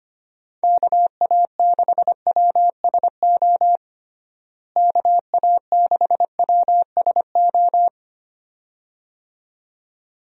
Morse code, heard as a keyed tone, about 25 wpm, 700 Hz. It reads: KA6WSO KA6WSO
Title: USA Call Signs - 6 Characters - Repeated 2x - 25wpm